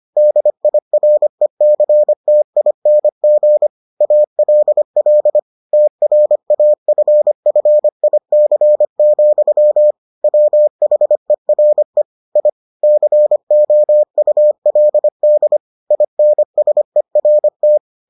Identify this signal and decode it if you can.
Morse code, heard as a keyed tone, about 25 wpm, 600 Hz. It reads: DIRECTING ALL TRAFFIC, WHERE I COULD INSERT